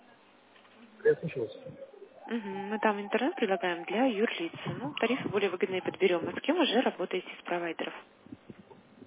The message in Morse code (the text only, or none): none